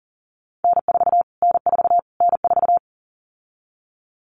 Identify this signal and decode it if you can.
Morse code, heard as a keyed tone, about 37 wpm, 700 Hz. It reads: N4 N4 N4
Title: Callsign Prefixes for Top 25 DXCC - Repeated 3x - 40wpm